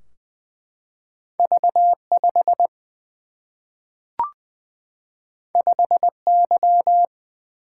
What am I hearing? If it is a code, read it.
Morse code, heard as a keyed tone, about 20 wpm, 700 Hz. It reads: V 5 5 Y
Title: Single-Letter Single-Number 20wpm